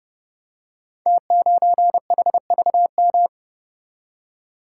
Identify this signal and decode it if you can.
Morse code, heard as a keyed tone, about 30 wpm, 700 Hz. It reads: T9HVM
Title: Code Groups - 5 Letter:Numbers - 30wpm